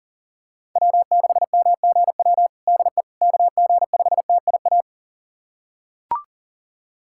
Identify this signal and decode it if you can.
Morse code, heard as a keyed tone, about 40 wpm, 700 Hz. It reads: W6MGW DE KG5TIA
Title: USA Callsign DE Callsign 40wpm